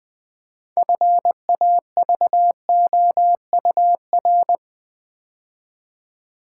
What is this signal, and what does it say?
Morse code, heard as a keyed tone, about 20 wpm, 700 Hz. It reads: FAVOUR
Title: 6 Letter Words - First 3 Letters Given - 20wpm